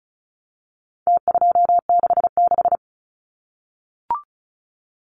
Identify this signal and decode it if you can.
Morse code, heard as a keyed tone, about 34 wpm, 700 Hz. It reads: T266
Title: Numbers - Contest Style - 4-Digit 35wpm